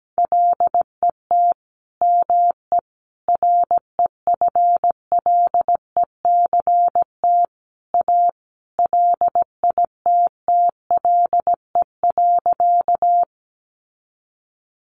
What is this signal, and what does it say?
Morse code, heard as a keyed tone, about 17 wpm, 700 Hz. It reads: LET ME REFLECT A LITTLE.
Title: Sentences from Top 1500 Words 17wpm